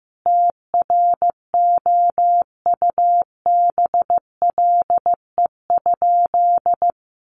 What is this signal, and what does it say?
Morse code, heard as a keyed tone, about 15 wpm, 700 Hz. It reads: TROUBLE?
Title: Sentences from Top 1200 Words 15wpm